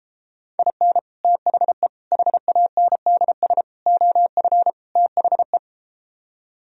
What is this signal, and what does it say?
Morse code, heard as a keyed tone, about 33 wpm, 700 Hz. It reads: IN THE HANDS OF THE